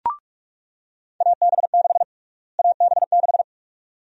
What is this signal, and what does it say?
Morse code, heard as a keyed tone, about 42 wpm, 690 Hz. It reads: AB6 AB6
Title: USA Call Sign Prefixes - Repeated 2x - 45wpm